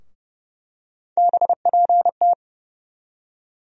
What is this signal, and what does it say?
Morse code, heard as a keyed tone, about 30 wpm, 700 Hz. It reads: BPT
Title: USA Call Sign Postfixes - 3 Letters 30wpm